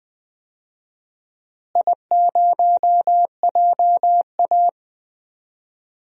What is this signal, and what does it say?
Morse code, heard as a keyed tone, about 20 wpm, 700 Hz. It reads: I0JA
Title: DXCC-100 Call Signs - 4 Characters - Repeated 3x - 20wpm